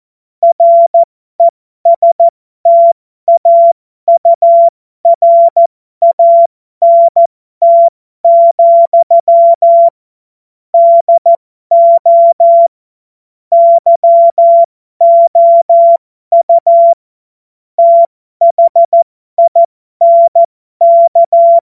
Morse code, heard as a keyed tone, about 14 wpm, spaced as for 12 wpm, 680 Hz. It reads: RESTAURANT, DO YOU THINK
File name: Morsekurs / Witze